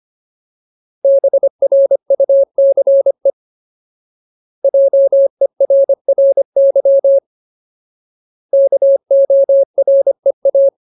Morse code, heard as a keyed tone, about 25 wpm, 550 Hz. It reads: BRUCE JERRY KOREA
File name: Ultimate Word List / 25WPM Ultimate